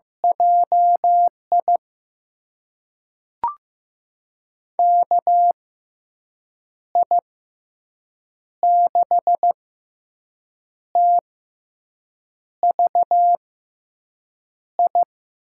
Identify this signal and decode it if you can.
Morse code, heard as a keyed tone, about 15 wpm, spaced as for 6 wpm, 700 Hz. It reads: JI KI6TVI
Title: USA Call Signs - 6x Character Spacing - 15wpm